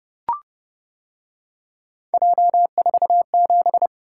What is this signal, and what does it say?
Morse code, heard as a keyed tone, about 29 wpm, 700 Hz. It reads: J47